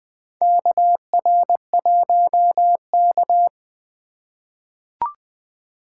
Morse code, heard as a keyed tone, about 20 wpm, 700 Hz. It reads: KR1K
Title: USA Call Signs - 4 Characters - 2x Character Spacing - 20wpm